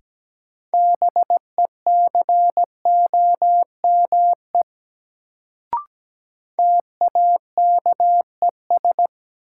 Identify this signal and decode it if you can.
Morse code, heard as a keyed tone, about 17 wpm, 700 Hz. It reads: BECOME TAKES
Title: Sets of 5 Words Encore - New Word Drill - 17wpm